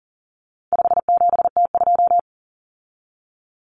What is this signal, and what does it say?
Morse code, heard as a keyed tone, about 40 wpm, 700 Hz. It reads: 57T2
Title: Common Cut Numbers - 4-Digit 40wpm